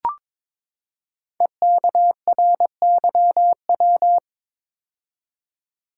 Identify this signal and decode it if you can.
Morse code, heard as a keyed tone, about 22 wpm, 690 Hz. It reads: EKRYW
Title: Code Groups - 5 Letters - 22wpm